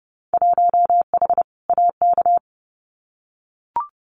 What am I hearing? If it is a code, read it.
Morse code, heard as a keyed tone, about 30 wpm, 700 Hz. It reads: 1H AK
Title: Winter Field Day Exchanges Encore 30wpm